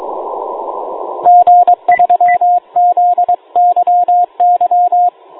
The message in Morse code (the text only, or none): G3ZYY